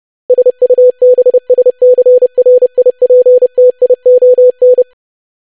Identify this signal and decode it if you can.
Morse code, heard as a keyed tone, about 30 wpm, 500 Hz. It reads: SUBSCRIPTION
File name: lesson / web / longwords-25wpm